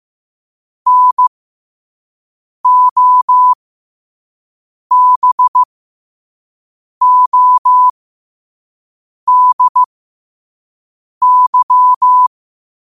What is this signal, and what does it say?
Morse code, heard as a keyed tone, about 15 wpm, spaced as for 5 wpm, 1000 Hz. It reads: NOBODY
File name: qotdfiles / 5wpm